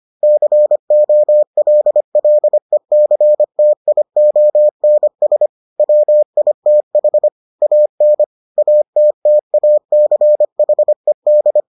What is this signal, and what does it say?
Morse code, heard as a keyed tone, about 25 wpm, 600 Hz. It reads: COLLECTIONS WITH AN ATTACHED